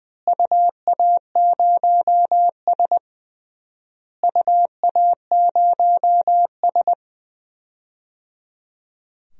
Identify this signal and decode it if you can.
Morse code, heard as a keyed tone, about 21 wpm, 700 Hz. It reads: UA0S UA0S
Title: DXCC-100 Call Signs - 4 Characters - Repeated 3x - 20wpm